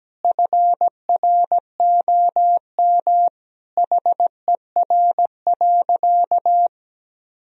Morse code, heard as a keyed tone, about 17 wpm, 700 Hz. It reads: FROM HER.
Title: Sentences from Top 1000 Words - Review Part 3 - 17wpm